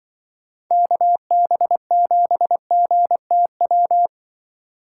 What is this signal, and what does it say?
Morse code, heard as a keyed tone, about 23 wpm, 700 Hz. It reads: KB7GTW
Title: Speed Racing - USA 2x3 Call Signs 20wpm